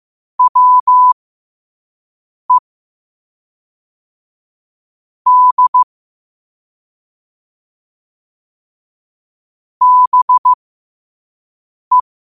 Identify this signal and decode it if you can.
Morse code, heard as a keyed tone, about 15 wpm, spaced as for 5 wpm, 1000 Hz. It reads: WE D BE
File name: qotdfiles / 5wpm